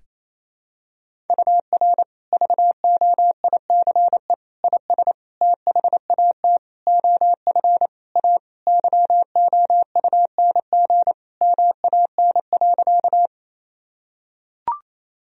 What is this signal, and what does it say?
Morse code, heard as a keyed tone, about 28 wpm, 700 Hz. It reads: UR VOICE IS THAT OF A YOUNG MAN.
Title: Sentences from Top 500 Words and Abbreviations 28wpm